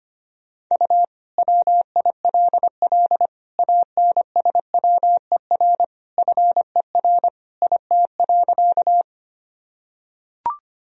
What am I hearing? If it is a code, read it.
Morse code, heard as a keyed tone, about 25 wpm, 700 Hz. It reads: U WILL ANSWER FER IT.
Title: Sentences from Top 300 Words and Abbreviations 25wpm